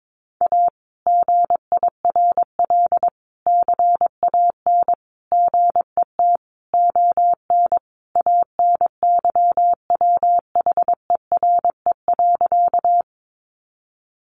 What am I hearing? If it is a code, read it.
Morse code, heard as a keyed tone, about 22 wpm, 700 Hz. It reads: A GIRL CAN GET ON ANYWHERE.
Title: Sentences from Top 1400 Words 22wpm